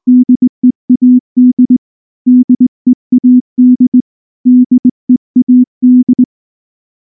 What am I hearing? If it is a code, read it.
Morse code, heard as a keyed tone, about 19 wpm, 250 Hz. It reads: DEAD DEAD DEAD
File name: morse / morse-training-20WPM-6